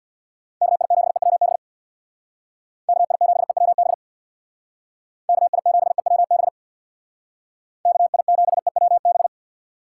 Morse code, heard as a keyed tone, about 62 wpm, 700 Hz. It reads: KI7EWB KI7EWB KI7EWB KI7EWB
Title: Speed Racing - USA 2x3 Call Signs 50wpm